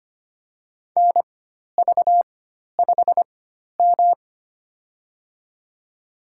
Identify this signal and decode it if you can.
Morse code, heard as a keyed tone, about 25 wpm, spaced as for 12 wpm, 700 Hz. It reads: NV5M
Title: USA Call Signs - 4 Characters - 4x Character Spacing - 25wpm